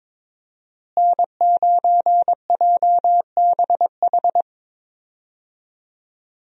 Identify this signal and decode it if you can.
Morse code, heard as a keyed tone, about 22 wpm, 700 Hz. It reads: N9JBH
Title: USA Call Signs - 5 Characters - Repeated 2x - 22wpm